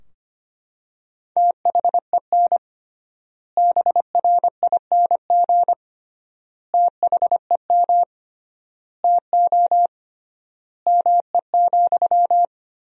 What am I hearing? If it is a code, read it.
Morse code, heard as a keyed tone, about 25 wpm, 700 Hz. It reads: THEN BRING THEM TO ME,
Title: 3x Word-Spacing - Sentences from Top 500 Words - 25wpm